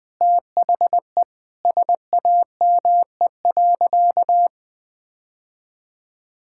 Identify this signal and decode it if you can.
Morse code, heard as a keyed tone, about 20 wpm, 700 Hz. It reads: THE SAME.
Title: Sentences with 2-Word Phrases - Spoken Beginning - 20wpm